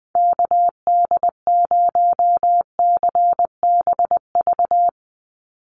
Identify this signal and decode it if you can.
Morse code, heard as a keyed tone, about 20 wpm, 700 Hz. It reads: KD0CBV
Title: USA Call Signs 20wpm